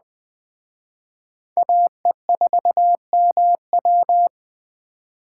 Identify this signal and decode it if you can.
Morse code, heard as a keyed tone, about 20 wpm, 700 Hz. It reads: AE4MW
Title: Speed Racing - USA Call Signs 17wpm